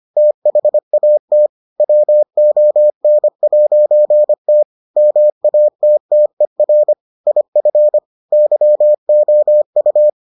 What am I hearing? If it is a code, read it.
Morse code, heard as a keyed tone, about 25 wpm, 600 Hz. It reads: THAT WON'T MATTER IF YOU